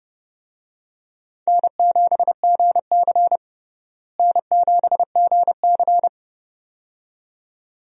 Morse code, heard as a keyed tone, about 30 wpm, 700 Hz. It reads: N7GC N7GC
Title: USA Call Signs - 4 Characters - Repeated 2x - 30wpm